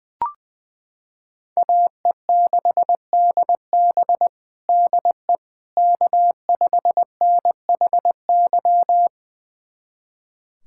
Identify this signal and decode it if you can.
Morse code, heard as a keyed tone, about 19 wpm, 700 Hz. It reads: AE6DB DE K5NHY
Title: USA Callsign DE Callsign - 5 Characters - 20wpm